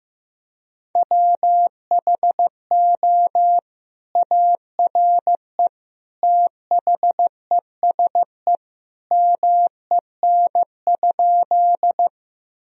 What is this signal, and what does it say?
Morse code, heard as a keyed tone, about 15 wpm, 700 Hz. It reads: WHO ARE THESE MEN?
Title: Sentences from Top 1000 Words - Review Part 1 - 15wpm